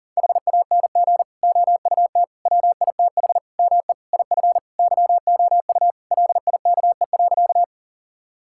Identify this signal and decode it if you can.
Morse code, heard as a keyed tone, about 40 wpm, 700 Hz. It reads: HANG OUT WITH ME IF YOU LIKE.
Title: Sentences from Top 1600 Words 40wpm